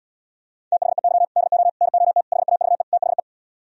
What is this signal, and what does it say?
Morse code, heard as a keyed tone, about 77 wpm, 700 Hz. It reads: THEY DO NOT HAVE THE